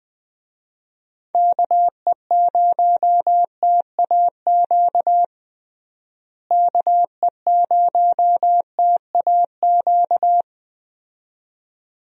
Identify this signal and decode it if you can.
Morse code, heard as a keyed tone, about 20 wpm, 700 Hz. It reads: KE0TAQ KE0TAQ
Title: USA Call Signs - 6 Characters - Repeated 2x - 20wpm